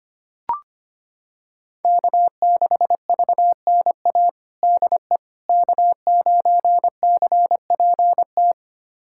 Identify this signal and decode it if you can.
Morse code, heard as a keyed tone, about 25 wpm, 700 Hz. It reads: K6VNA DE K9CPT
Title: USA Callsign DE Callsign - 5 Characters - 25wpm